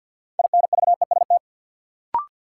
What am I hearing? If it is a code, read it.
Morse code, heard as a keyed tone, about 52 wpm, 700 Hz. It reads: INVEST
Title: Sentences from Top 1100 Words - New Word Drill Alt - 50wpm